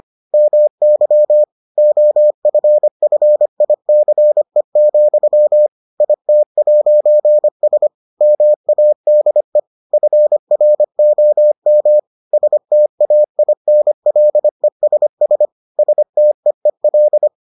Morse code, heard as a keyed tone, about 25 wpm, 600 Hz. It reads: MY OFFICE, IT'S MADE FROM STAINLESS STEEL